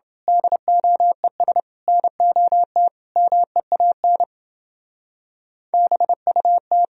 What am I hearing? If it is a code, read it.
Morse code, heard as a keyed tone, about 28 wpm, 700 Hz. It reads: DOES NOT MEAN BUT